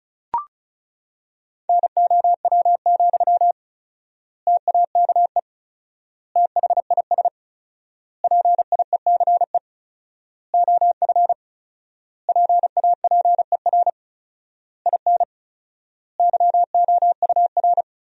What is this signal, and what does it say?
Morse code, heard as a keyed tone, about 36 wpm, 700 Hz. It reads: NOW, TAKE THIS PIECE OF PAPER IN YOUR
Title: 4x Word-Spacing - Sentences from Top 400 Words - 35wpm